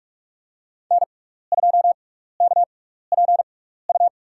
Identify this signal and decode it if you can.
Morse code, heard as a keyed tone, about 47 wpm, spaced as for 16 wpm, 700 Hz. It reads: N2KPU
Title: USA Call Signs - 5 Characters - 6x Character Spacing - 45wpm